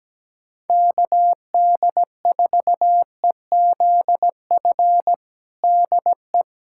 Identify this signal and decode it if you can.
Morse code, heard as a keyed tone, about 17 wpm, 700 Hz. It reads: KD4EZF DE
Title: USA Callsign DE Callsign 17wpm